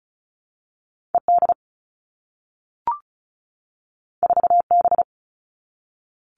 Morse code, heard as a keyed tone, about 36 wpm, 700 Hz. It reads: ED 4B
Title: All Cut Numbers - 2-Digit 35wpm